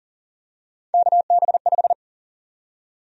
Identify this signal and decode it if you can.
Morse code, heard as a keyed tone, about 38 wpm, 700 Hz. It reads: KB5
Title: USA Call Sign Prefixes - 40wpm